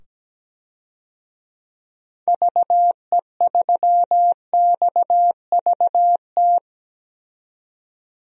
Morse code, heard as a keyed tone, about 17 wpm, 700 Hz. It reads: VE3XVT
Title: DXCC-100 Call Signs - 6 Characters - 17wpm